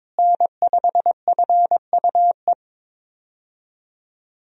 Morse code, heard as a keyed tone, about 21 wpm, 700 Hz. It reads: N5FUE
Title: USA Call Signs - 5 Characters - Repeated 2x - 22wpm